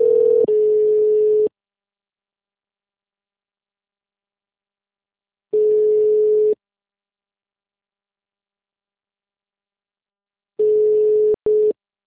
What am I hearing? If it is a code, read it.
Ringback tone (Europe, 425 Hz).
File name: out- -2042- - - .71230